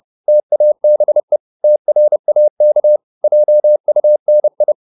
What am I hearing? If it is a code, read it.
Morse code, heard as a keyed tone, about 30 wpm, 600 Hz. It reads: TABE TRAK JUNI
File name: Korte ord DK 30wpm